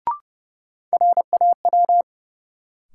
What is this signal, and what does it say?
Morse code, heard as a keyed tone, about 30 wpm, 700 Hz. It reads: RAW
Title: Sentences from Top 1500 Words - New Word Drill - 30wpm